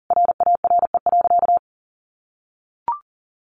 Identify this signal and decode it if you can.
Morse code, heard as a keyed tone, about 40 wpm, 700 Hz. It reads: RARE.
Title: Sentences from Top 1500 Words 40wpm